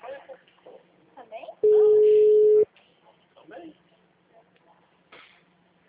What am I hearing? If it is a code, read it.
Ringback tone (Europe, 425 Hz), with speech.